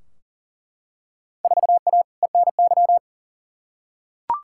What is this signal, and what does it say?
Morse code, heard as a keyed tone, about 40 wpm, 700 Hz. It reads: 4A ENY
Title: Field Day Exchanges - 40wpm